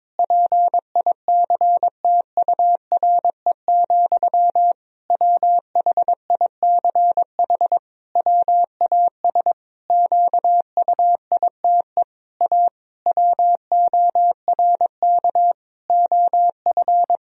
Morse code, heard as a keyed tone, about 22 wpm, 700 Hz. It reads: PICTURE, WHICH WAS QUITE A WORK OF
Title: Sentences from Top 1000 Words - Review Part 3 - 22wpm